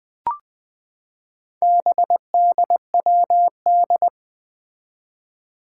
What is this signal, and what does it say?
Morse code, heard as a keyed tone, about 20 wpm, 700 Hz. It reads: BDWD
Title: Four Challenging Letter-Number Combinations 20wpm